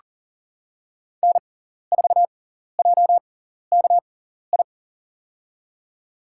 Morse code, heard as a keyed tone, about 40 wpm, spaced as for 14 wpm, 700 Hz. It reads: N4JKI
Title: USA Call Signs - 5 Characters - 6x Character Spacing - 40wpm